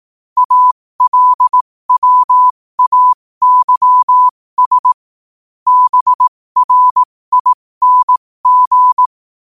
Morse code, heard as a keyed tone, about 18 wpm, spaced as for 15 wpm, 1000 Hz. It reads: ALWAYS BRING
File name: qotdfiles / 15wpm